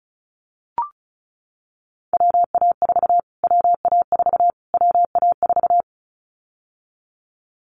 Morse code, heard as a keyed tone, about 34 wpm, 700 Hz. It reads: WA4 WA4 WA4
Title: Callsign Prefixes for Top 25 DXCC - Repeated 3x - 35wpm